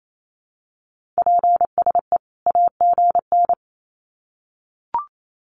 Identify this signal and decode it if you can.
Morse code, heard as a keyed tone, about 28 wpm, 700 Hz. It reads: PSE AGN
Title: Common QSO Phrases 28wpm